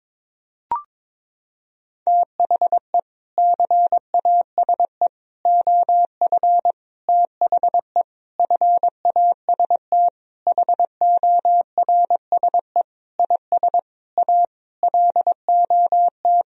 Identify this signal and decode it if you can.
Morse code, heard as a keyed tone, about 22 wpm, 700 Hz. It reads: THE CASE OF THE FAST HORSE IS A LOT
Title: Sentences from Top 1000 Words - Review Part 2 - 22wpm